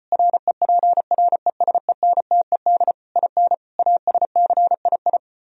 Morse code, heard as a keyed tone, about 33 wpm, 700 Hz. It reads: REPRESENTED IN ASCII